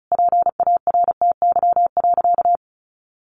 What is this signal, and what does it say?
Morse code, heard as a keyed tone, about 34 wpm, 700 Hz. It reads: PARTY.
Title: Sentences from Top 1400 Words 35wpm